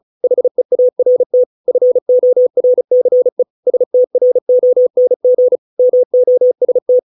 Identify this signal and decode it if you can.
Morse code, heard as a keyed tone, about 34 wpm, 500 Hz. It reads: HEART FORCE STRONG MOST